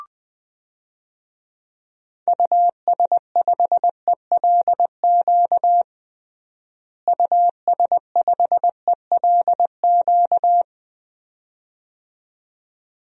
Morse code, heard as a keyed tone, about 20 wpm, 700 Hz. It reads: US5ELQ US5ELQ
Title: DXCC-100 Call Signs - 6 Characters - Repeated 2x - 20wpm